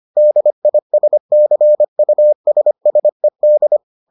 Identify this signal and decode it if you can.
Morse code, heard as a keyed tone, about 25 wpm, 600 Hz. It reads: DISCUSSED